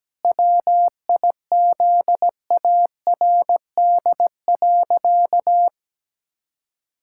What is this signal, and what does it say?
Morse code, heard as a keyed tone, about 17 wpm, 700 Hz. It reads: WIZARD.
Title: Pangram Challenge v2 - 17wpm